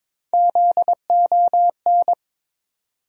Morse code, heard as a keyed tone, about 22 wpm, 700 Hz. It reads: ZON
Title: USA Call Sign Postfixes - 3 Letters 22wpm